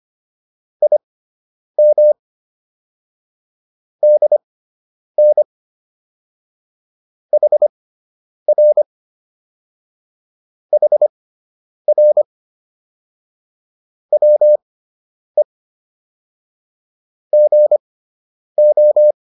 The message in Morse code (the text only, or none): IM DN HR HR WE GO